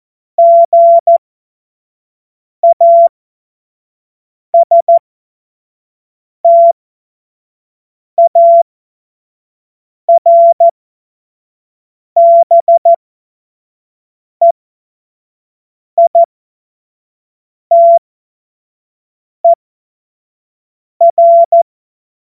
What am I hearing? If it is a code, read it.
Morse code, heard as a keyed tone, about 14 wpm, spaced as for 5 wpm, 680 Hz. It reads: GASTARBEITER
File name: Morsekurs / Witze / Witz 10.Oktober 2023 25 F70